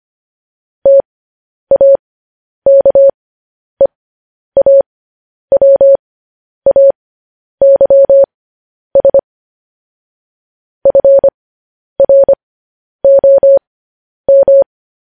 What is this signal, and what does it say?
Morse code, heard as a keyed tone, about 25 wpm, spaced as for 10 wpm, 550 Hz. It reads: TAKEAWAYS FROM